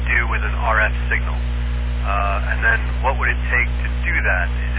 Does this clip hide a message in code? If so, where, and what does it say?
Yes: Emergency Alert System (SAME) tones from 0.7 to 4.5 s, heard as harsh buzzing bursts, with speech.